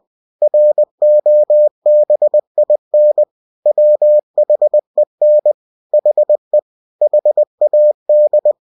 Morse code, heard as a keyed tone, about 20 wpm, 600 Hz. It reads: ROBIN WHEN HE HAD
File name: cw-books / winnie-the-pooh / winnie-20wpm-page-0098